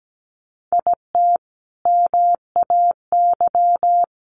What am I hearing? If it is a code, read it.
Morse code, heard as a keyed tone, about 17 wpm, 700 Hz. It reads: IT MAY
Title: Sentences from Top 1300 Words 17wpm